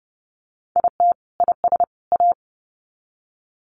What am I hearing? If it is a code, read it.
Morse code, heard as a keyed tone, about 30 wpm, 700 Hz. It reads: IT IS A
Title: Sets of 3 Words 30wpm